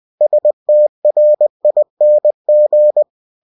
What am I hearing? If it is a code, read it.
Morse code, heard as a keyed tone, about 20 wpm, 600 Hz. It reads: STRING
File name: cw-books / winnie-the-pooh / winnie-20wpm-page-0109